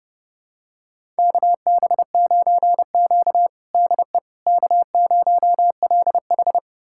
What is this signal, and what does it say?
Morse code, heard as a keyed tone, about 30 wpm, 700 Hz. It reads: KB9Q DE K0LH
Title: USA Callsign DE Callsign - 4 Characters - 30wpm